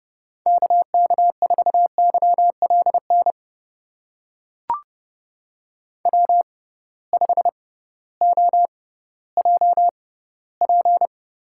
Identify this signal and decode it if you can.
Morse code, heard as a keyed tone, about 30 wpm, spaced as for 16 wpm, 700 Hz. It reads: KK4YLN W5OJP
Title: USA Call Signs - 6x Character Spacing - 30wpm